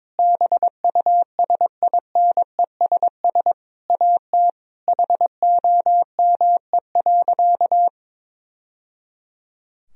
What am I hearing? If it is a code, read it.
Morse code, heard as a keyed tone, about 22 wpm, 700 Hz. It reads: BUSINESS AT HOME.
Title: Sentences from Top 1000 Words - Review Part 4 - 22wpm